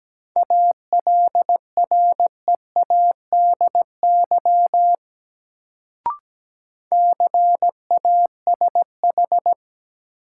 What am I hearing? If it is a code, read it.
Morse code, heard as a keyed tone, about 17 wpm, 700 Hz. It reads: ALREADY CASH
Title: Sentences from Top 700 Words - New Word Drill - 17wpm